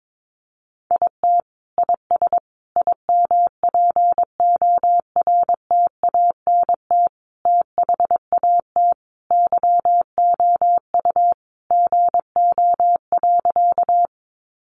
Morse code, heard as a keyed tone, about 22 wpm, 700 Hz. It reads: IT IS IMPORTANT THAT YOU GO.
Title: Sentences from Top 600 Words 22wpm